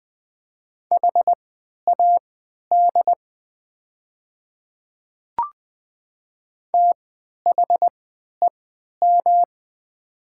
Morse code, heard as a keyed tone, about 20 wpm, spaced as for 11 wpm, 700 Hz. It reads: HAD THEM
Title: 3x Character Spacing - Top 100 Words - 20wpm